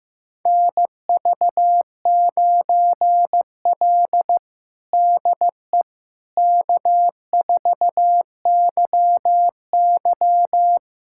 Morse code, heard as a keyed tone, about 16 wpm, 700 Hz. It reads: NV9L DE K4YY
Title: USA Callsign DE Callsign - 4 Characters - 15wpm